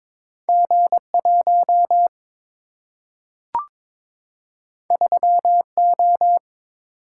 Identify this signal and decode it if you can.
Morse code, heard as a keyed tone, about 22 wpm, 700 Hz. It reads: G1 3O
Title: Single-Letter Single-Number 22wpm